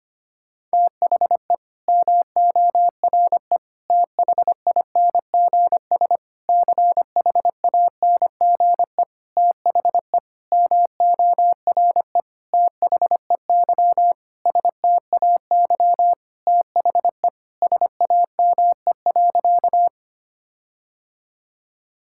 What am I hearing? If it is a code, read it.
Morse code, heard as a keyed tone, about 25 wpm, 700 Hz. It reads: THE MORE THINGS CHANGE THE MORE THEY STAY THE SAME.
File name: practice / 25wpm / Sentences with 9-Word Phrases - 25wpm